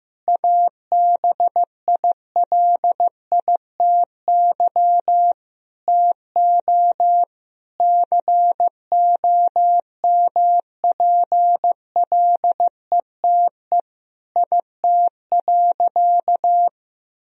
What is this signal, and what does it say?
Morse code, heard as a keyed tone, about 15 wpm, 700 Hz. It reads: ABILITY TO COMPLETE IT.
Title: Sentences from Top 1000 Words - Review Part 3 - 15wpm